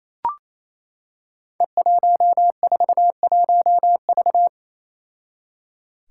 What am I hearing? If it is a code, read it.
Morse code, heard as a keyed tone, about 28 wpm, 700 Hz. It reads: E141V
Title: Five Challenging Letter-Number Combinations 28wpm